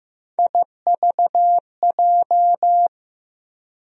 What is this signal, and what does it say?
Morse code, heard as a keyed tone, about 15 wpm, 700 Hz. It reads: IVJ
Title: USA Call Sign Postfixes - 3 Letters 15wpm